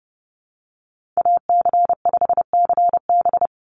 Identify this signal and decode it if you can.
Morse code, heard as a keyed tone, about 30 wpm, 700 Hz. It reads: AC5CB
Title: USA Call Signs 30wpm